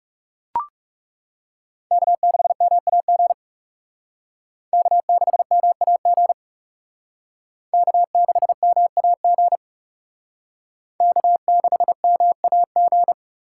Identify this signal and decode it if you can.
Morse code, heard as a keyed tone, about 38 wpm, 700 Hz. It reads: K6MAG K6MAG K6MAG K6MAG
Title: Speed Racing - USA 1x3 Call Signs 30wpm